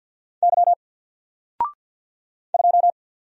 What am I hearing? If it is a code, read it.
Morse code, heard as a keyed tone, about 49 wpm, 700 Hz. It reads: Y 2
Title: Single Character (Letter:Number:Punctuation:Prosign) 50wpm